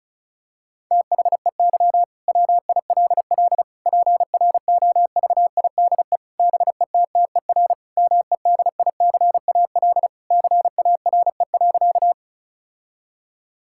Morse code, heard as a keyed tone, about 36 wpm, 700 Hz. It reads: THEY WILL PROVIDE BETTER MEDICAL CARE.
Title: Sentences from Top 1000 Words - Review Part 5 - 35wpm